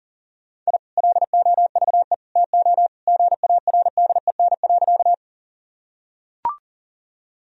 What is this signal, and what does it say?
Morse code, heard as a keyed tone, about 40 wpm, 700 Hz. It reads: I LOVE TO GARDEN.
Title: Sentences from Top 1000 Words - Review Part 4 - 40wpm